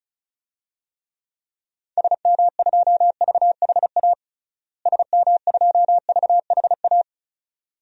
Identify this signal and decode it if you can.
Morse code, heard as a keyed tone, about 36 wpm, 700 Hz. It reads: SM2VHA SM2VHA
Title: DXCC-100 Call Signs - 6 Characters - Repeated 2x - 35wpm